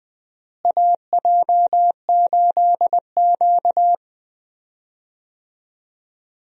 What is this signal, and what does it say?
Morse code, heard as a keyed tone, about 20 wpm, 700 Hz. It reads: AJ8Q
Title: USA Call Signs - 4 Characters - Repeated 3x - 20wpm